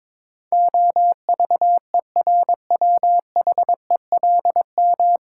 Morse code, heard as a keyed tone, about 22 wpm, 700 Hz. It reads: OVERWHELM
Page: Morse Code Ninja practice 22wpm